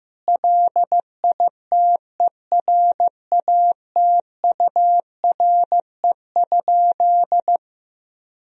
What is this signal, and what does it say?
Morse code, heard as a keyed tone, about 15 wpm, 700 Hz. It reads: LITERATURE?
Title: Sentences from Top 1000 Words - Review Part 3 - 15wpm